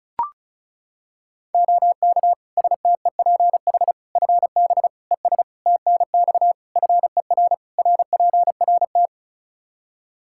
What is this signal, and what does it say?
Morse code, heard as a keyed tone, about 36 wpm, 700 Hz. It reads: OK STEPH FB ES TNX FER RPRT
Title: Practice QSO - Exchange 2 - Info - 35wpm